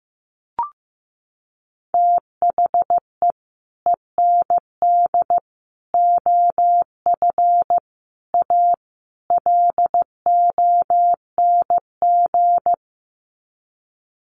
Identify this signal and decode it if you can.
Morse code, heard as a keyed tone, about 15 wpm, 700 Hz. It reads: THE END OF A LONG
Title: Sets of 5 Words Encore2 15wpm